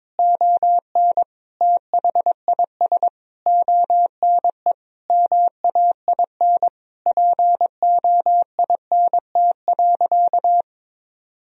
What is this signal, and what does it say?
Morse code, heard as a keyed tone, about 22 wpm, 700 Hz. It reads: ON THIS ONE MAIN POINT.
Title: Sentences from Top 1000 Words - Review Part 2 - 22wpm